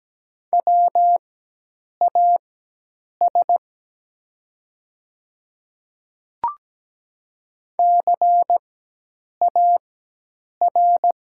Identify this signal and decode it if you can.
Morse code, heard as a keyed tone, about 17 wpm, spaced as for 8 wpm, 700 Hz. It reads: WAS CAR
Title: 4x Character Spacing - Top 300 Words - 17wpm